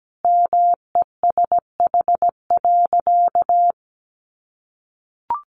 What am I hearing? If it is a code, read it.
Morse code, heard as a keyed tone, about 17 wpm, 700 Hz. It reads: MESH.
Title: A Sound Of Thunder 17wpm